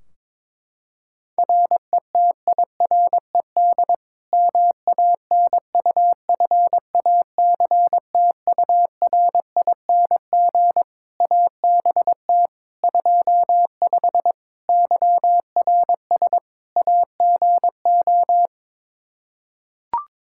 Morse code, heard as a keyed tone, about 21 wpm, 700 Hz. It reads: RETIRED MANUFACTURING ABT 25 YRS AGO
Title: K1USN SST QSO Practice 22wpm